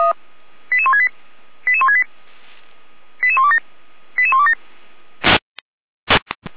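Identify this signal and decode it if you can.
Five-tone selective call (ZVEI), heard as quick runs of five tones, tones of about 70 ms. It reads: 88168 88168 88127 88127
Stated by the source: ZVEI five-tone call received by SDR.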